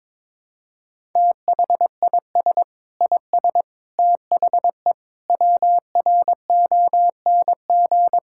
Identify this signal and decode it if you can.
Morse code, heard as a keyed tone, about 21 wpm, 700 Hz. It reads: THIS IS THE WRONG
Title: Sentences from Top 1000 Words - Review Part 6 - 22wpm